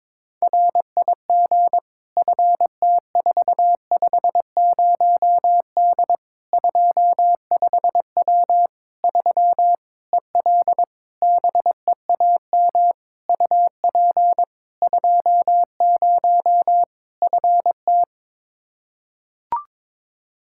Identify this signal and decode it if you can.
Morse code, heard as a keyed tone, about 22 wpm, 700 Hz. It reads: RIG FT450D 25W 3 EL BEAM UP 20 FT
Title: Practice QSO - Exchange 2 - Info - 22wpm